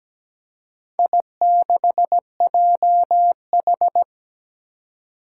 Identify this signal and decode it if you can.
Morse code, heard as a keyed tone, about 17 wpm, 700 Hz. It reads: I6JH
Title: Four Challenging Letter-Number Combinations 17wpm